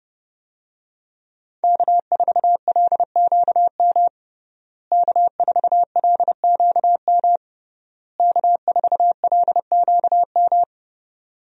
Morse code, heard as a keyed tone, about 30 wpm, 700 Hz. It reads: K4LQM K4LQM K4LQM
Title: USA Call Signs - 5 Characters - Repeated 3x - 30wpm